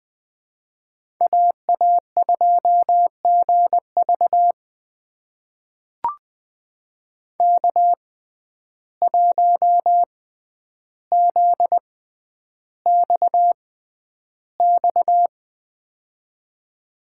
Morse code, heard as a keyed tone, about 21 wpm, spaced as for 10 wpm, 700 Hz. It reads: AA2GV K1ZXX
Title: USA Call Signs - 5 Characters - 6x Character Spacing - 20wpm